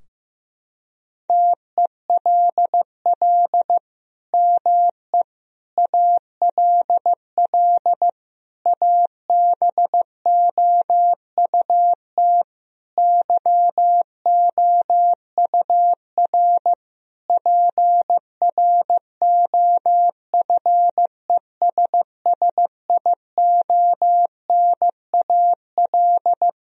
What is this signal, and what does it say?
Morse code, heard as a keyed tone, about 15 wpm, 700 Hz. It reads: TELL ME ALL ABOUT YOUR PROFESSIONAL
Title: Sentences from Top 1000 Words - Review Part 4 - 15wpm